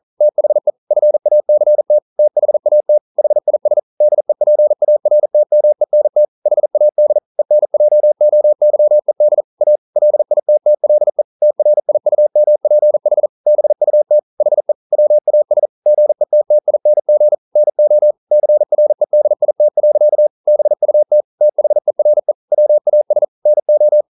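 Morse code, heard as a keyed tone, about 40 wpm, 600 Hz. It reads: THE FACT THAT HIS DEPARTMENT HAD ENJOYED A LITTLE TRIUMPH BUT HE WAS GETTING NO CREDIT. BUT THERE WAS NO
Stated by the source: Morse code practice at 41 wpm, from CW med Gnister.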